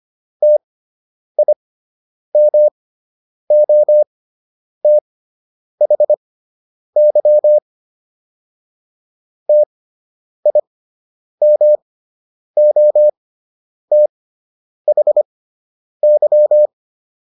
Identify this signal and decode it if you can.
Morse code, heard as a keyed tone, about 25 wpm, spaced as for 9 wpm, 600 Hz. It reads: TIMOTHY TIMOTHY